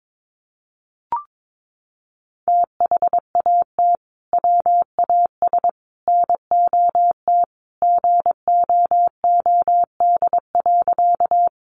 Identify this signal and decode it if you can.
Morse code, heard as a keyed tone, about 22 wpm, 700 Hz. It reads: THAT WAS NOT GOOD.
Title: Sentences from Top 600 Words 22wpm